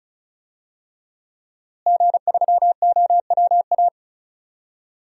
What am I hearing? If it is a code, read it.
Morse code, heard as a keyed tone, about 36 wpm, 700 Hz. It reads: G3OWA
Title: DXCC-100 Call Signs - 5 Characters - Repeated 3x - 35wpm